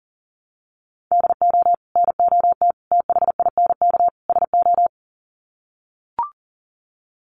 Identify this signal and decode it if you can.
Morse code, heard as a keyed tone, about 40 wpm, 700 Hz. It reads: DO NOT THINK SO
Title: Sets of 4 Words 40wpm